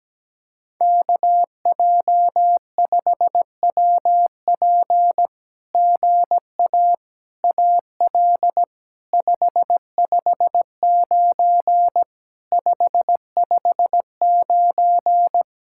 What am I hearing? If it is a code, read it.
Morse code, heard as a keyed tone, about 17 wpm, 700 Hz. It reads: KJ5WP GA AL 559 559
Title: SOTA QSO Practice 17wpm